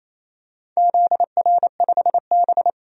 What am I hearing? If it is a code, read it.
Morse code, heard as a keyed tone, about 28 wpm, 700 Hz. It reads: ZR5B
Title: DXCC-100 Call Signs - 4 Characters - 28wpm